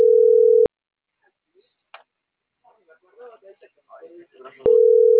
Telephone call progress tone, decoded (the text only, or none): ringback tone